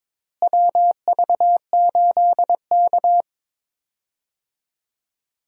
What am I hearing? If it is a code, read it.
Morse code, heard as a keyed tone, about 22 wpm, 700 Hz. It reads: WV8K
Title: USA Call Signs - 4 Characters - Repeated 3x - 22wpm